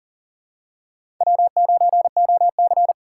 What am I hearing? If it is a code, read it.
Morse code, heard as a keyed tone, about 41 wpm, 690 Hz. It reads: W9OC
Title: USA Call Signs - 4 Characters - 2x Character Spacing - 40wpm